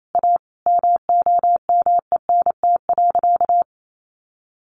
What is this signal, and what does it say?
Morse code, heard as a keyed tone, about 28 wpm, 700 Hz. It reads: A MOMENT.
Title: Sentences from Top 1000 Words 28wpm